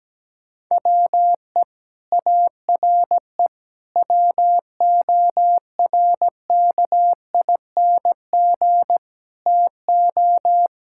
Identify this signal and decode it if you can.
Morse code, heard as a keyed tone, about 17 wpm, 700 Hz. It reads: WE ARE WORKING TO
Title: Sentences from Top 1000 Words - Review Part 6 - 17wpm